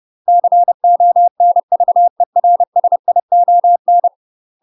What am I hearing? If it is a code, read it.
Morse code, heard as a keyed tone, about 30 wpm, 700 Hz. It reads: CONVERSION
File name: UK MASTER QSOs TRAINING/30WPM